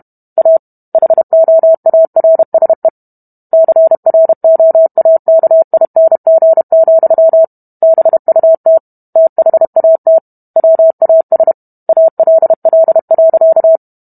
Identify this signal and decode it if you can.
Morse code, heard as a keyed tone, about 32 wpm, 650 Hz. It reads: A HOARSE CROAKING, BUT THAT WAS ALL.